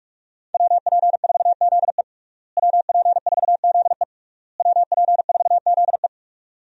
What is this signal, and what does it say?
Morse code, heard as a keyed tone, about 42 wpm, 700 Hz. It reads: WP4ZE WP4ZE WP4ZE
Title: USA Call Signs - 5 Characters - Repeated 3x - 45wpm